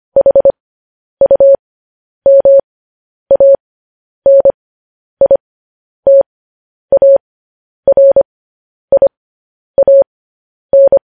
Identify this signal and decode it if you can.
Morse code, heard as a keyed tone, about 25 wpm, spaced as for 10 wpm, 550 Hz. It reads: HUMANITARIAN
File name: podcast / News headlines-1x-25x10